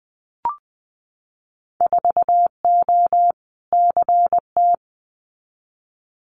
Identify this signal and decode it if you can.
Morse code, heard as a keyed tone, about 19 wpm, 700 Hz. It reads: 4O CT
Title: Winter Field Day Exchanges Encore 20wpm